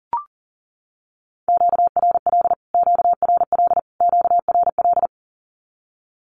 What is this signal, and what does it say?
Morse code, heard as a keyed tone, about 40 wpm, 700 Hz. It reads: QRL QRL QRL
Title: Q-Codes Repeated 3x 40wpm